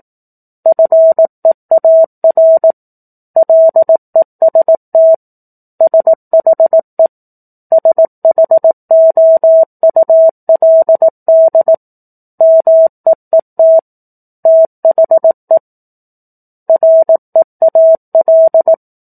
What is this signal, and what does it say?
Morse code, heard as a keyed tone, about 18 wpm, 650 Hz. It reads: FEAR LEST SHE SHOULD MEET THE REAL
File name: 04-Chapter-04 18 WPM